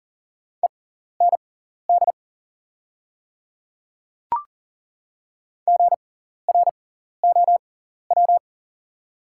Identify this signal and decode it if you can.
Morse code, heard as a keyed tone, about 40 wpm, spaced as for 14 wpm, 700 Hz. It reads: END GROW
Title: 6x Character Spacing - Top 300 Words - 40wpm